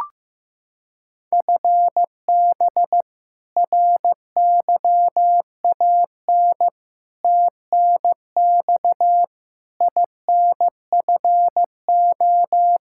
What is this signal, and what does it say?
Morse code, heard as a keyed tone, about 16 wpm, 700 Hz. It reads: FB RYAN TNX INFO
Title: Practice QSO - Exchange 3 - More Info - 15wpm